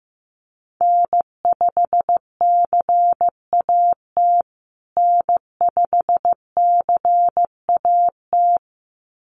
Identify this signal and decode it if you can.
Morse code, heard as a keyed tone, about 15 wpm, 700 Hz. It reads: N5CAT N5CAT
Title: USA Call Signs Repeated 2x 15wpm